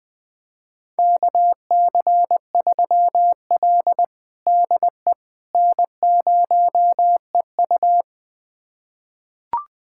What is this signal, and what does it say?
Morse code, heard as a keyed tone, about 21 wpm, 700 Hz. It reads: KC3L DE N0EU
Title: USA Callsign DE Callsign - 4 Characters - 20wpm